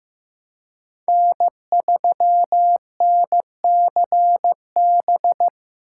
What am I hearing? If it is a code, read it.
Morse code, heard as a keyed tone, about 15 wpm, 700 Hz. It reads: N3NCB
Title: USA Call Signs - 5 Characters - 2x Character Spacing - 15wpm